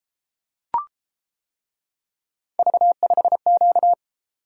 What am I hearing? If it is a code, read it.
Morse code, heard as a keyed tone, about 31 wpm, 700 Hz. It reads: V5Q